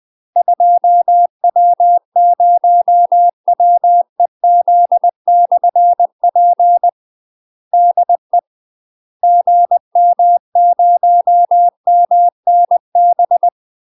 Morse code, heard as a keyed tone, about 20 wpm, 700 Hz. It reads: 2W0WEZ/P DE GM0MNB
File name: UK MASTER QSOs TRAINING / 20WPM / UK MASTER QSOs 0042